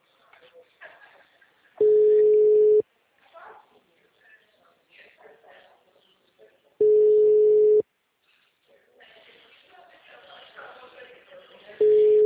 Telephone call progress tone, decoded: ringback tone

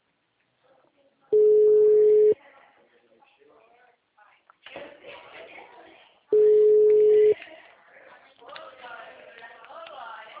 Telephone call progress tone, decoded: ringback tone

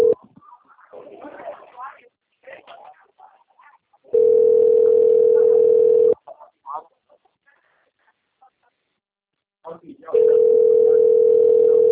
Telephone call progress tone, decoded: ringback tone

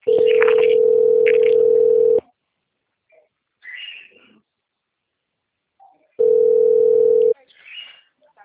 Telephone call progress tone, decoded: ringback tone

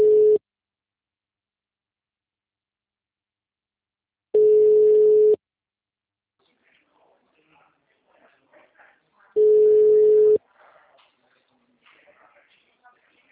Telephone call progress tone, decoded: ringback tone

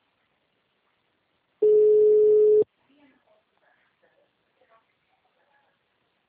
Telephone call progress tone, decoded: ringback tone

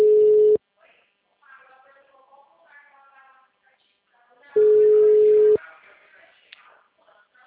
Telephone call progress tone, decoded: ringback tone

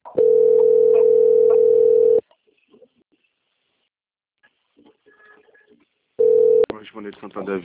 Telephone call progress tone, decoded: ringback tone

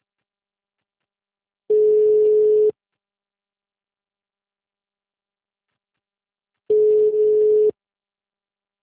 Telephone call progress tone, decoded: ringback tone

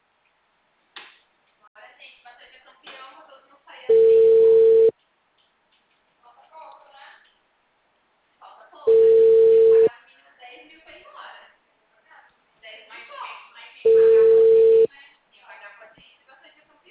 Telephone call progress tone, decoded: ringback tone

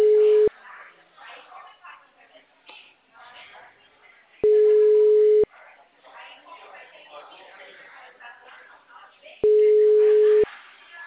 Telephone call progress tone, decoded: ringback tone

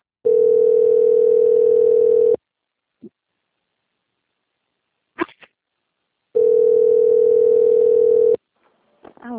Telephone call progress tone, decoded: ringback tone

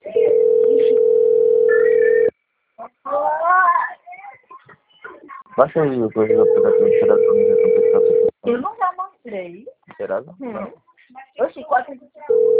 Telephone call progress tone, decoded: ringback tone

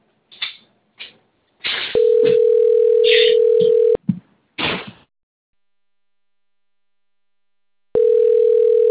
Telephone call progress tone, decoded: ringback tone

